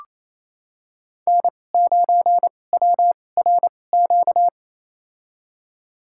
Morse code, heard as a keyed tone, about 28 wpm, spaced as for 20 wpm, 700 Hz. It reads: N9WRQ
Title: USA Call Signs - 5 Characters - 2x Character Spacing - 28wpm